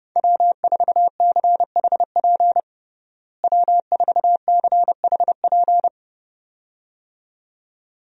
Morse code, heard as a keyed tone, about 32 wpm, 700 Hz. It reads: W4CHP W4CHP
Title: USA Call Signs - 5 Characters - Repeated 2x - 30wpm